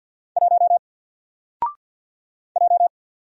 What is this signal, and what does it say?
Morse code, heard as a keyed tone, about 50 wpm, 700 Hz. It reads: 1 J
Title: Single Character (Letter:Number:Punctuation:Prosign) 50wpm